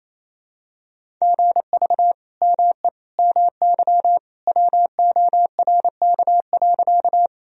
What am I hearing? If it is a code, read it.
Morse code, heard as a keyed tone, about 28 wpm, 700 Hz. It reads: GV ME MY WORK.
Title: Sentences from Top 200 Words and Abbreviations 28wpm